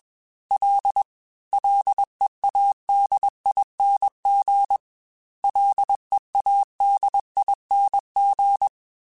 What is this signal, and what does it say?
Morse code, heard as a keyed tone, about 21 wpm, 800 Hz. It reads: L LEADING LEADING
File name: Ls / Single L m3s / Single L @20